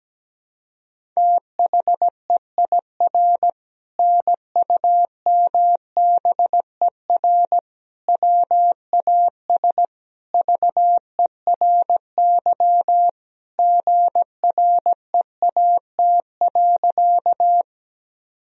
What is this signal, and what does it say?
Morse code, heard as a keyed tone, about 17 wpm, 700 Hz. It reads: THEIR NUMBER WAS VERY GREAT.
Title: Sentences from Top 1000 Words - Review Part 1 - 17wpm